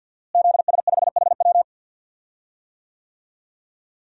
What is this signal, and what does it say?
Morse code, heard as a keyed tone, about 52 wpm, 700 Hz. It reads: ZS5HW
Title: DXCC-100 Call Signs - 5 Characters - Repeated 3x - 50wpm